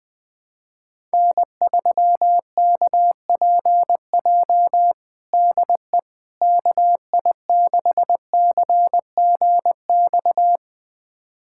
Morse code, heard as a keyed tone, about 21 wpm, 700 Hz. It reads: N3KPJ DE KI6CGX
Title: USA Callsign DE Callsign 20wpm